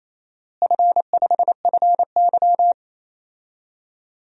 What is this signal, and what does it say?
Morse code, heard as a keyed tone, about 28 wpm, 700 Hz. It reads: F5FY
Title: DXCC-100 Call Signs - 4 Characters - Repeated 2x - 28wpm